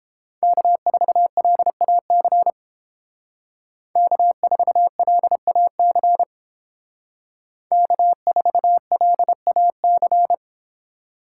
Morse code, heard as a keyed tone, about 31 wpm, 700 Hz. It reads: K4LAC K4LAC K4LAC